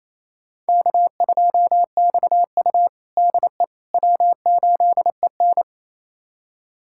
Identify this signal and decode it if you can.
Morse code, heard as a keyed tone, about 28 wpm, 700 Hz. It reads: K2XU DE W8EN
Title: USA Callsign DE Callsign - 4 Characters - 28wpm